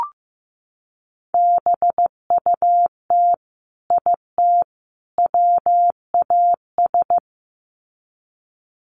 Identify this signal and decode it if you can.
Morse code, heard as a keyed tone, about 15 wpm, 700 Hz. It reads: BUT IT WAS
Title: Sets of 3 Words 15wpm